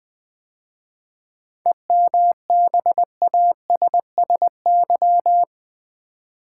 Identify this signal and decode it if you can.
Morse code, heard as a keyed tone, about 20 wpm, 700 Hz. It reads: EMBASSY
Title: 7 Letter Words - First 3 Letters Given - 20wpm